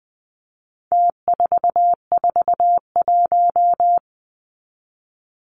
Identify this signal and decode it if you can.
Morse code, heard as a keyed tone, about 20 wpm, 700 Hz. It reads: T441
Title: Numbers - Contest Style - 4-Digit 20wpm